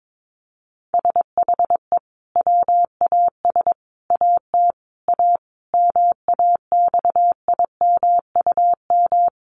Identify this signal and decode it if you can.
Morse code, heard as a keyed tone, about 22 wpm, 700 Hz. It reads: SHE WAS AT A MAXIMUM
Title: Sentences from Top 1300 Words 22wpm